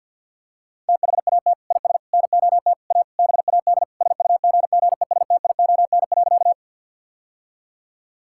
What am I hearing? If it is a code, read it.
Morse code, heard as a keyed tone, about 50 wpm, 700 Hz. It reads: THAT IS NOT A BAD SUGGESTION.
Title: Sentences from Top 1600 Words 50wpm